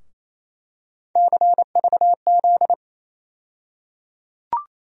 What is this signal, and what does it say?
Morse code, heard as a keyed tone, about 28 wpm, 700 Hz. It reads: CVZ
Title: USA Call Sign Postfixes - 3 Letters 28wpm